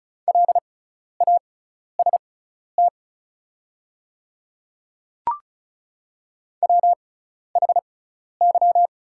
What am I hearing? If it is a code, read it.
Morse code, heard as a keyed tone, about 36 wpm, spaced as for 12 wpm, 700 Hz. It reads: LAST WHY
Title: 6x Character Spacing - Top 300 Words - 35wpm